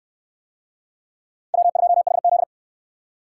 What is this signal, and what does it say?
Morse code, heard as a keyed tone, about 66 wpm, 700 Hz. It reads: W2HZ